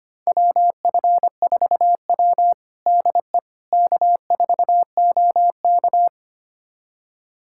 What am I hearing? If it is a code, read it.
Morse code, heard as a keyed tone, about 25 wpm, 700 Hz. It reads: WF4W DE K4OK
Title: USA Callsign DE Callsign - 4 Characters - 25wpm